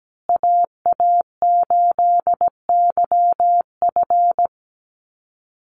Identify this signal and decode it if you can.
Morse code, heard as a keyed tone, about 17 wpm, 700 Hz. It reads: AA8YF
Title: USA Call Signs Repeated 3x 17wpm